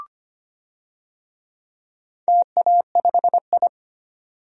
Morse code, heard as a keyed tone, about 25 wpm, 700 Hz. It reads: TA5I
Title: DXCC-100 Call Signs - 4 Characters - 25wpm